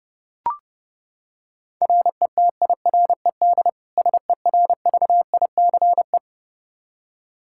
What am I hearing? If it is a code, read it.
Morse code, heard as a keyed tone, about 31 wpm, 700 Hz. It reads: RETIRED SERVICE
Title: Practice QSO - Exchange 3 - More Info - 30wpm